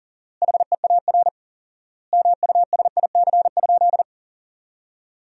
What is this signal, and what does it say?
Morse code, heard as a keyed tone, about 40 wpm, 700 Hz. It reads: HEAR MUSIC?
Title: 4x Word-Spacing - Sentences from Top 300 Words - 40wpm